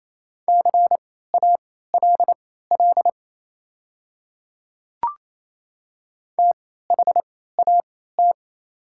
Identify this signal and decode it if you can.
Morse code, heard as a keyed tone, about 28 wpm, spaced as for 16 wpm, 700 Hz. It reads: CALL THAT
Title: 3x Character Spacing - Top 200 Words - 28wpm